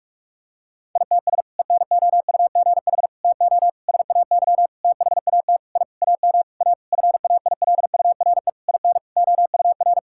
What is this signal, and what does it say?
Morse code, heard as a keyed tone, about 46 wpm, 700 Hz. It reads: ITS ENOUGH TO SAY THAT I AM A FAILURE IN OUR
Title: Sentences from Top 1000 Words - Review Part 6 - 45wpm